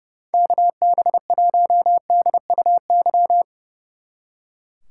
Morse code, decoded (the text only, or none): KB1DUY